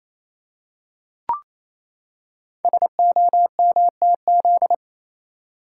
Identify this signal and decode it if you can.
Morse code, heard as a keyed tone, about 29 wpm, 700 Hz. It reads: SOMTZ
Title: Code Groups - 5 Letter:Numbers - 28wpm